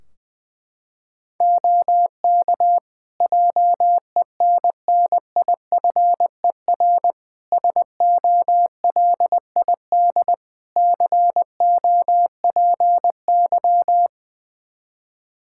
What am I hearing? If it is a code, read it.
Morse code, heard as a keyed tone, about 21 wpm, 700 Hz. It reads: OK JENNIFER SOLID COPY
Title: Practice QSO - Exchange 3 - More Info - 20wpm